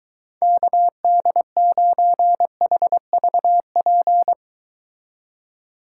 Morse code, heard as a keyed tone, about 22 wpm, 700 Hz. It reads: KD9HVP